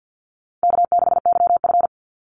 Morse code, heard as a keyed tone, about 50 wpm, 700 Hz. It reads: K6YF
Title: Easy Call Signs - 50wpm